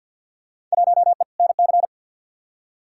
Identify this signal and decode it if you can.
Morse code, heard as a keyed tone, about 52 wpm, 700 Hz. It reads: 1E NC
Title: Field Day Exchanges - 50wpm